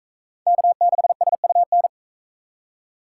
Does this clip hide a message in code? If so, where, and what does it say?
Yes: Morse code from 0.5 to 1.9 s, heard as a keyed tone, about 39 wpm, 700 Hz. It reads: K6SUN